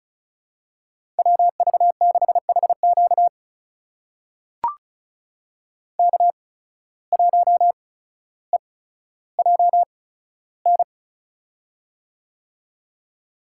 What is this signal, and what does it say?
Morse code, heard as a keyed tone, about 36 wpm, spaced as for 15 wpm, 700 Hz. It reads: WV6HQ K1EJN
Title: USA Call Signs - 5 Characters - 8x Character Spacing - 35wpm